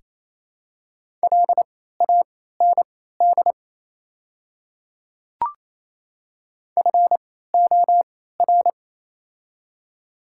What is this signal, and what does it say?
Morse code, heard as a keyed tone, about 28 wpm, spaced as for 16 wpm, 700 Hz. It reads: LAND FOR
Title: 3x Character Spacing - Top 500 Words - 28wpm